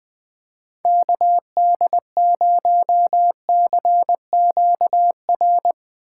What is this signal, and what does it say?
Morse code, heard as a keyed tone, about 20 wpm, 700 Hz. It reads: KD0CQR
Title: USA Call Signs - 3x Character Spacing - 20wpm